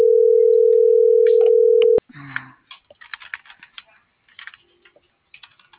Ringback tone (North America), 440 + 480 Hz.